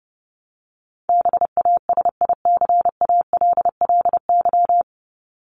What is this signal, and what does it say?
Morse code, heard as a keyed tone, about 28 wpm, 700 Hz. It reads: BASICALLY
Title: Top 1251-1500 Words 30wpm